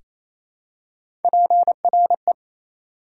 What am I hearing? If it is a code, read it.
Morse code, heard as a keyed tone, about 28 wpm, 700 Hz. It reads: PRE
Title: Prefixes and Suffixes 28wpm